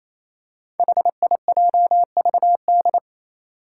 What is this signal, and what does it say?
Morse code, heard as a keyed tone, about 27 wpm, 700 Hz. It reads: HIJVD
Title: Five Challenging Letter-Number Combinations 28wpm